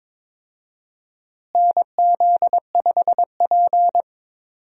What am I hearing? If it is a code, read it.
Morse code, heard as a keyed tone, about 22 wpm, 700 Hz. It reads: NZ5P
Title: USA Call Signs - 4 Characters - Repeated 2x - 22wpm